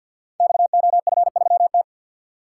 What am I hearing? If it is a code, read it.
Morse code, heard as a keyed tone, about 48 wpm, 700 Hz. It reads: XOF3T
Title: Code Groups - 5 Letter:Numbers - 50wpm